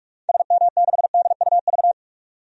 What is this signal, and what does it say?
Morse code, heard as a keyed tone, about 42 wpm, 700 Hz. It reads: SM6DUV
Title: DXCC-100 Call Signs - 6 Characters - 45wpm